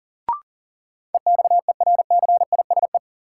Morse code, heard as a keyed tone, about 37 wpm, 700 Hz. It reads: EXERCISE
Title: Sentences from Top 1200 Words - New Word Drill Alt - 40wpm